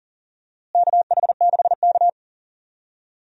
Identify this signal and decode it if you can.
Morse code, heard as a keyed tone, about 40 wpm, 700 Hz. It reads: KH6K
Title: DXCC-100 Call Signs - 4 Characters - Repeated 3x - 40wpm